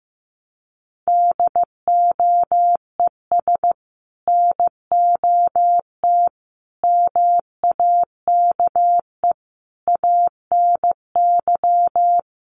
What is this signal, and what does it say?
Morse code, heard as a keyed tone, about 15 wpm, 700 Hz. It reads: DOES NOT MAKE ANY
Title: Sets of 5 Words Encore2 15wpm